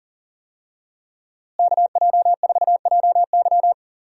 Morse code, heard as a keyed tone, about 39 wpm, 700 Hz. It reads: KJ4JY
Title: USA Call Signs - 5 Characters - Repeated 2x - 40wpm